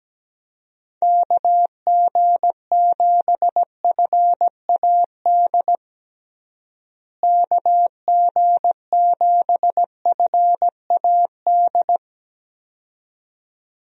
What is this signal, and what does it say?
Morse code, heard as a keyed tone, about 17 wpm, 700 Hz. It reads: KG7FAD KG7FAD
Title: USA Call Signs - 6 Characters - Repeated 2x - 17wpm